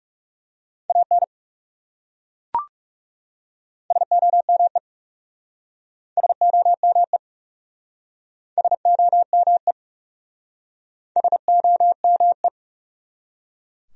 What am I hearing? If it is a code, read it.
Morse code, heard as a keyed tone, about 37 wpm, 700 Hz. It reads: AN SOME SOME SOME SOME
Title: Speed-Racing Top 100 Words 30wpm